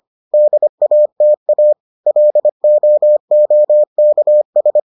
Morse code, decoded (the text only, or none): DATA LOOKS